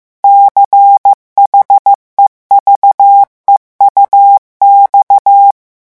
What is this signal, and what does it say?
Morse code, heard as a keyed tone, about 15 wpm, 800 Hz. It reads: CHEVEUX
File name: morse mp3 / revision9 citations humoristiques vitesse 14 mots minute